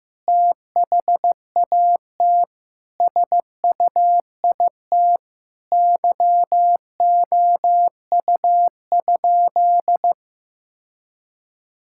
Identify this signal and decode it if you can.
Morse code, heard as a keyed tone, about 15 wpm, 700 Hz. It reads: THAT SUIT YOU?
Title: Sentences from Top 1600 Words 15wpm